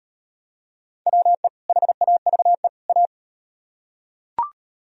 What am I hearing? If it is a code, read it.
Morse code, heard as a keyed tone, about 37 wpm, 700 Hz. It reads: WE HAVE A